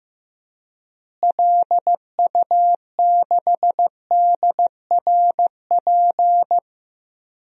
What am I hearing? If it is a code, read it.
Morse code, heard as a keyed tone, about 15 wpm, 700 Hz. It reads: LU6DRP
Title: DXCC-100 Call Signs - 6 Characters - Repeated 3x - 15wpm